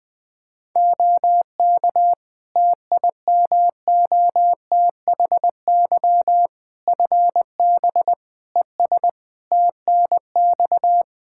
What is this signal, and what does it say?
Morse code, heard as a keyed tone, about 20 wpm, 700 Hz. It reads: OK TIMOTHY FB ES TNX
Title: Practice QSO - Exchange 2 - Info - 20wpm